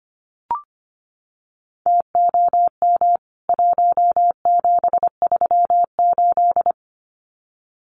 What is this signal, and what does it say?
Morse code, heard as a keyed tone, about 25 wpm, 700 Hz. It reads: TOM 1738
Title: CWT Exchange 25wpm